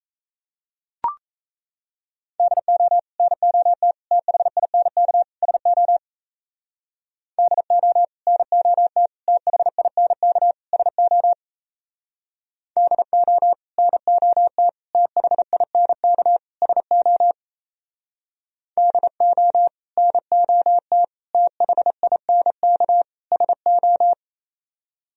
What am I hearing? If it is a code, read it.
Morse code, heard as a keyed tone, about 35 wpm, 700 Hz. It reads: DO NOT THINK SO DO NOT THINK SO DO NOT THINK SO DO NOT THINK SO
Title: Speed-Racing - Sets of 4 Words 28wpm